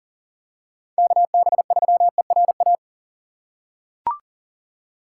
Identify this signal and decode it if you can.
Morse code, heard as a keyed tone, about 40 wpm, 700 Hz. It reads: KB3ERA
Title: USA Call Signs - 6 Characters - 8x Character Spacing - 40wpm